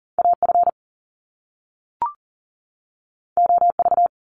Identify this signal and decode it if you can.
Morse code, heard as a keyed tone, about 40 wpm, 700 Hz. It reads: AF OV
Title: Top Two Letter Combinations in Words 40wpm